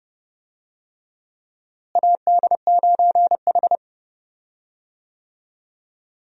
Morse code, heard as a keyed tone, about 30 wpm, 700 Hz. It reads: AD9H
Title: USA Call Signs - 4 Characters - Repeated 3x - 30wpm